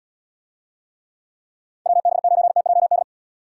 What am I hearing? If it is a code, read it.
Morse code, heard as a keyed tone, about 77 wpm, 700 Hz. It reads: WB9TJD